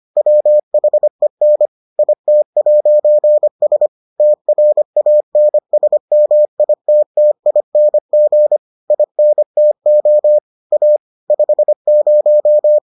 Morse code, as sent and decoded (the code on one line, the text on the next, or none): .-- .... -. / .. - .----. ... / - .-. .- -. ... -- .. - - .. -. --. / .. -. - --- / .- / ..... -----
WHEN IT'S TRANSMITTING INTO A 50